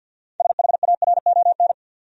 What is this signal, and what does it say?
Morse code, heard as a keyed tone, about 53 wpm, 700 Hz. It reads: SHARON
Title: USA Common Names - 50wpm